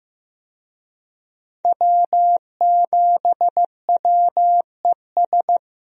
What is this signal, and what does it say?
Morse code, heard as a keyed tone, about 15 wpm, 700 Hz. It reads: W7WES
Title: USA Call Signs - 5 Characters - Repeated 2x - 15wpm